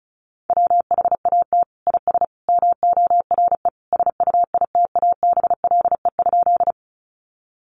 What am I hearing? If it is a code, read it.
Morse code, heard as a keyed tone, about 34 wpm, 700 Hz. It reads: WHAT IS MORE SUITABLE?
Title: Sentences from Top 1400 Words 35wpm